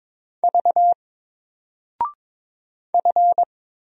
Morse code, heard as a keyed tone, about 22 wpm, 700 Hz. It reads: V F
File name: Single-Letter-Number-22wpm